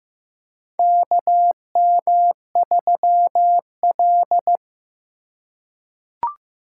Morse code, heard as a keyed tone, about 15 wpm, 700 Hz. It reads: KM3L
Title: USA Call Signs - 4 Characters - 2x Character Spacing - 15wpm